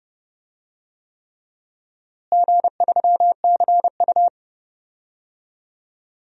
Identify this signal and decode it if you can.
Morse code, heard as a keyed tone, about 30 wpm, 700 Hz. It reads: G3CU
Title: DXCC-100 Call Signs - 4 Characters - Repeated 3x - 30wpm